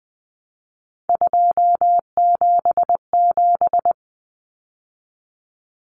Morse code, heard as a keyed tone, about 20 wpm, 700 Hz. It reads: 277
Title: Common Cut Numbers - 3-Digit 20wpm